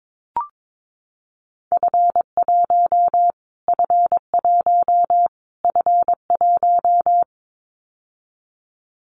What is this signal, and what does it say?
Morse code, heard as a keyed tone, about 22 wpm, 700 Hz. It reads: F1 F1 F1
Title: Callsign Prefixes for Top 25 DXCC - Repeated 3x - 22wpm